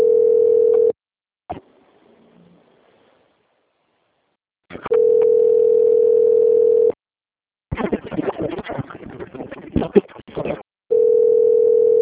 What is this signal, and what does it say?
Ringback tone (North America), 440 + 480 Hz.